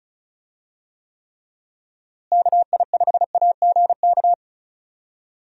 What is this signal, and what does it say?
Morse code, heard as a keyed tone, about 36 wpm, 700 Hz. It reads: KI5AGK
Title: USA Call Signs - 6 Characters - Repeated 2x - 35wpm